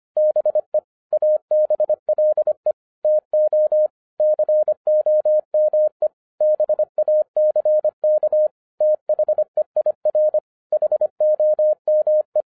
Morse code, heard as a keyed tone, about 25 wpm, 610 Hz. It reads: BE ABLE TO COME BACK THEIR HOME
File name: 2011-07-mp3 / n39 THE 3.11 2 25wpm